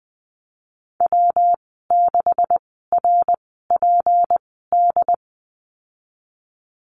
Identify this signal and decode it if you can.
Morse code, heard as a keyed tone, about 20 wpm, spaced as for 15 wpm, 700 Hz. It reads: W6RPD
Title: 154 Lesson 25g - Letter p - Call Signs 20wpm